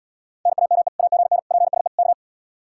Morse code, heard as a keyed tone, about 80 wpm, 700 Hz. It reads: NAME AGN PSE K